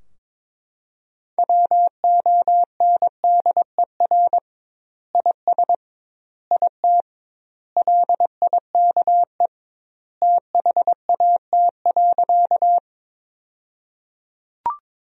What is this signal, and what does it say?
Morse code, heard as a keyed tone, about 22 wpm, 700 Hz. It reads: WONDER IS IT LIKE THAT.
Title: 2x Word-Spacing - Sentences from Top 500 Words - 22wpm